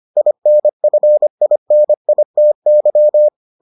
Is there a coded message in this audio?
Yes: Morse code from 0.2 to 3.3 s, heard as a keyed tone, about 25 wpm, 600 Hz. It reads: INFINITY